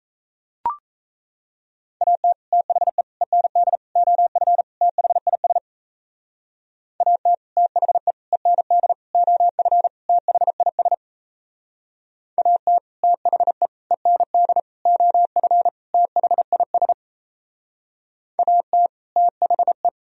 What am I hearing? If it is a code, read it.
Morse code, heard as a keyed tone, about 34 wpm, 700 Hz. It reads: AT THE END OF THIS AT THE END OF THIS AT THE END OF THIS AT THE
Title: Speed-Racing - Sets of 5 Words 28wpm